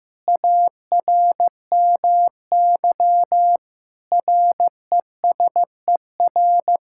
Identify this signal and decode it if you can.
Morse code, heard as a keyed tone, about 15 wpm, 700 Hz. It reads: ARMY RESER
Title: Sentences from Top 1000 Words - Review Part 6 - 15wpm